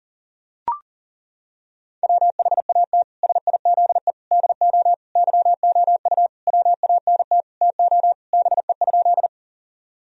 Morse code, heard as a keyed tone, about 40 wpm, 700 Hz. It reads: WHAT SIZE DO YOU WANT TO BE?
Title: Sentences from Top 1000 Words - Review Part 2 - 40wpm